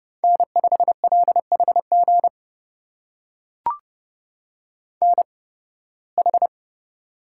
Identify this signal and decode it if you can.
Morse code, heard as a keyed tone, about 30 wpm, spaced as for 20 wpm, 700 Hz. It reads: N5LHG NH
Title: USA Call Signs - 5 Characters - 8x Character Spacing - 30wpm